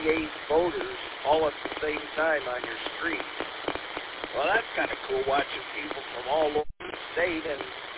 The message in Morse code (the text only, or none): none